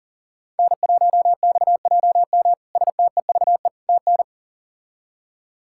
Morse code, heard as a keyed tone, about 40 wpm, 700 Hz. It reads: N1XJM STEVE TN
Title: K1USN SST QSO Practice - 40wpm